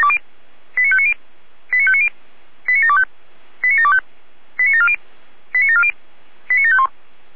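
Five-tone selective call (ZVEI), heard as quick runs of five tones, tones of about 70 ms. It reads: 89590 89590 89725 89725 89740 89740 89751
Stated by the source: ZVEI five-tone call 89735 received by SDR.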